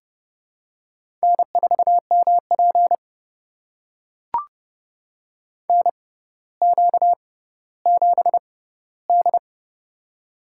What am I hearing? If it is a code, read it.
Morse code, heard as a keyed tone, about 29 wpm, spaced as for 15 wpm, 700 Hz. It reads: N4MP NQ7D
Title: USA Call Signs - 4 Characters - 6x Character Spacing - 30wpm